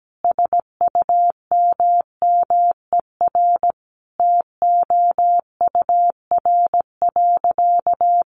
Morse code, heard as a keyed tone, about 17 wpm, 700 Hz. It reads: SUMMER TOUR.
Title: Sentences from Top 1400 Words 17wpm